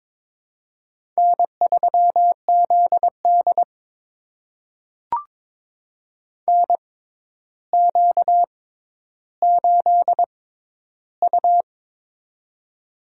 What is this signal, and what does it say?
Morse code, heard as a keyed tone, about 23 wpm, spaced as for 11 wpm, 700 Hz. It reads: N3ZD NQ8U
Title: USA Call Signs - 4 Characters - 6x Character Spacing - 22wpm